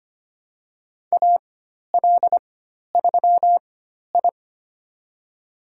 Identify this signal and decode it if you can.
Morse code, heard as a keyed tone, about 25 wpm, spaced as for 12 wpm, 700 Hz. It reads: AL3I
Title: USA Call Signs - 4 Characters - 4x Character Spacing - 25wpm